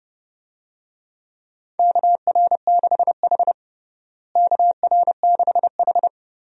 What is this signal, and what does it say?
Morse code, heard as a keyed tone, about 30 wpm, 700 Hz. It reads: KR6H KR6H
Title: USA Call Signs - 4 Characters - Repeated 3x - 30wpm